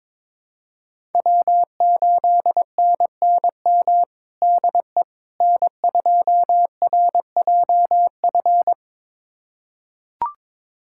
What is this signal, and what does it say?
Morse code, heard as a keyed tone, about 22 wpm, 700 Hz. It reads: W8NNM DE N2RJF
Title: USA Callsign DE Callsign - 5 Characters - 22wpm